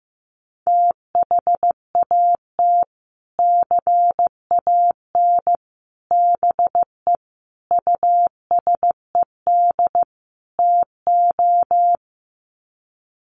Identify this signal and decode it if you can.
Morse code, heard as a keyed tone, about 15 wpm, 700 Hz. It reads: THAT CAN BE USED TO
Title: Sets of 5 Words Encore 15wpm